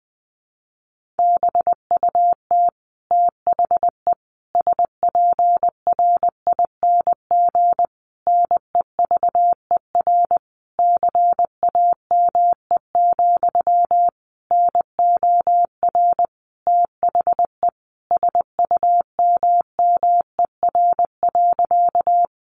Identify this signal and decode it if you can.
Morse code, heard as a keyed tone, about 20 wpm, 700 Hz. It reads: BUT THE SPRING NEVER CAME, NOR THE SUMMER.
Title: Sentences from Top 1500 Words 20wpm